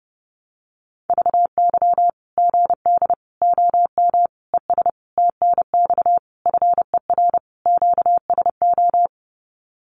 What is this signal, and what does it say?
Morse code, heard as a keyed tone, about 30 wpm, 700 Hz. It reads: VY GD OM ES TNX FER QSO
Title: Common QSO Phrases 30wpm